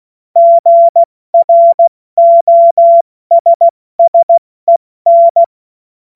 Morse code, heard as a keyed tone, about 16 wpm, 680 Hz. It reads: GROSSEN
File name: Witz-16-September-2025-70-F80